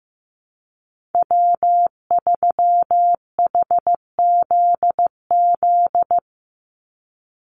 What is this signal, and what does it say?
Morse code, heard as a keyed tone, about 15 wpm, 700 Hz. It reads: W3HZZ
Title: Call Signs Repeated 3x - 15wpm